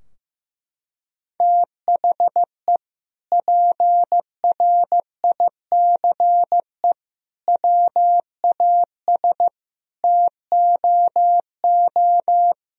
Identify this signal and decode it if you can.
Morse code, heard as a keyed tone, about 15 wpm, 700 Hz. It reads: THE PRICE WAS TOO
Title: Sentences from Top 1000 Words - Review Part 4 - 15wpm